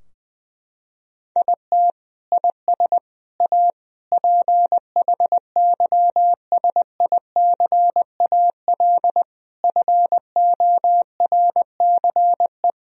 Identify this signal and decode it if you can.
Morse code, heard as a keyed tone, about 20 wpm, 700 Hz. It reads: IT IS A PHYSICAL FORCE
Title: Sentences from Top 1000 Words - Review Part 4 - 20wpm